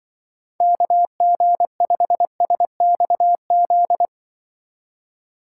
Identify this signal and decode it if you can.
Morse code, heard as a keyed tone, about 24 wpm, 700 Hz. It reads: KG5SXZ
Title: Speed Racing - USA 2x3 Call Signs 20wpm